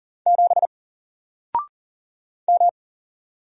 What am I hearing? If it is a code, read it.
Morse code, heard as a keyed tone, about 40 wpm, 700 Hz. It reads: 7 M
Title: Single Character (Letter:Number:Punctuation:Prosign) 40wpm